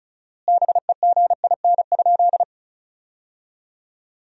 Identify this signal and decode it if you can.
Morse code, heard as a keyed tone, about 36 wpm, 700 Hz. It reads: BEGIN?
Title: 4x Word-Spacing - Sentences from Top 300 Words - 35wpm